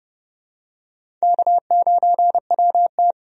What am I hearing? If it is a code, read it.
Morse code, heard as a keyed tone, about 30 wpm, 700 Hz. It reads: K9WT
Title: USA Call Signs - 4 Characters - 30wpm